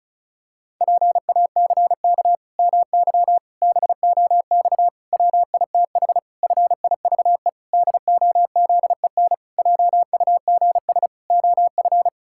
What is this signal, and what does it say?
Morse code, heard as a keyed tone, about 34 wpm, 700 Hz. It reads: PACK MY BOX WITH FIVE DOZEN JUGS OF
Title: Pangram Challenge v2 - 35wpm